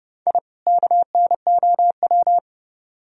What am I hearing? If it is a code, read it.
Morse code, heard as a keyed tone, about 28 wpm, 700 Hz. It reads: I KNOW